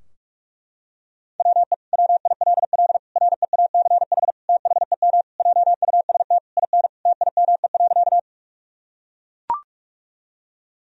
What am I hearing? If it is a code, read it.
Morse code, heard as a keyed tone, about 41 wpm, 700 Hz. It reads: WE WILL REACH THEM JUST IN TIME.
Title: Sentences from Top 1000 Words - Review Part 2 - 45wpm